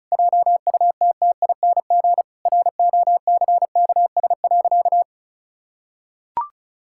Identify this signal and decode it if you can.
Morse code, heard as a keyed tone, about 34 wpm, 700 Hz. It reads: JUTTING ROCKS.
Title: Pangram Challenge v2 - 35wpm